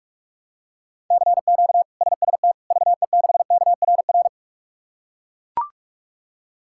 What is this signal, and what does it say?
Morse code, heard as a keyed tone, about 41 wpm, 700 Hz. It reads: CQ SST VE6KRR
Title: K1USN SST QSO Practice - 45wpm